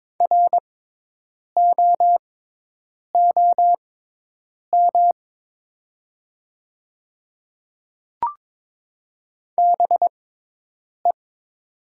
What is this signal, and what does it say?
Morse code, heard as a keyed tone, about 22 wpm, spaced as for 8 wpm, 700 Hz. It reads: ROOM BE
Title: 6x Character Spacing - Top 300 Words - 22wpm